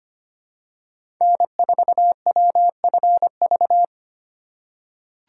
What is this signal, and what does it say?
Morse code, heard as a keyed tone, about 25 wpm, 700 Hz. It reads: N4WFV
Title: USA Call Signs - 5 Characters - 4x Character Spacing - 25wpm